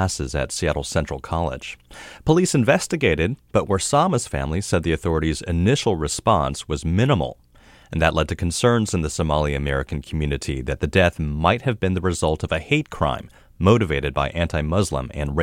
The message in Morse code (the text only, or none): none